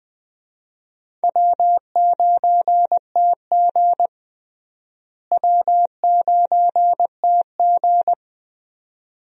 Morse code, heard as a keyed tone, about 20 wpm, 700 Hz. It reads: W9TG W9TG
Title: USA Call Signs - 4 Characters - Repeated 2x - 20wpm